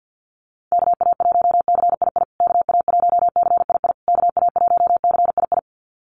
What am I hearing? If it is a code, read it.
Morse code, heard as a keyed tone, about 50 wpm, 700 Hz. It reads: KA1CII KA1CII KA1CII
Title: USA Call Signs Repeated 3x 50wpm